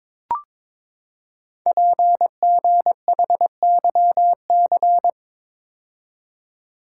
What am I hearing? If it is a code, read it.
Morse code, heard as a keyed tone, about 22 wpm, 700 Hz. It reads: PGHYC
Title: Code Groups - 5 Letters - 22wpm